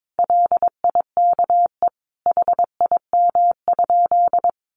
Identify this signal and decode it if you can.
Morse code, heard as a keyed tone, about 22 wpm, 700 Hz. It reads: LIKE HIM?
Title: Sentences from Top 100 Words 22wpm